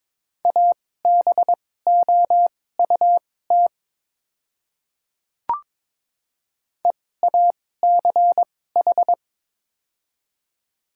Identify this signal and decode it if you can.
Morse code, heard as a keyed tone, about 22 wpm, spaced as for 16 wpm, 700 Hz. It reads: ABOUT EACH
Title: 2x Character Spacing - Top 100 Words - 22wpm